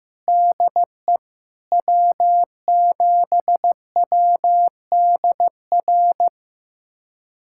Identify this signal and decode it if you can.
Morse code, heard as a keyed tone, about 15 wpm, 700 Hz. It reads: DE W7WDR
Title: USA Callsign DE Callsign - 5 Characters - 15wpm